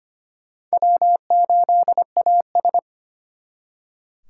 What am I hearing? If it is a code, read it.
Morse code, heard as a keyed tone, about 32 wpm, 700 Hz. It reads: W8AS